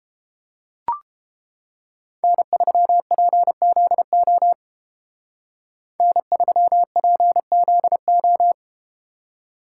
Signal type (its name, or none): Morse code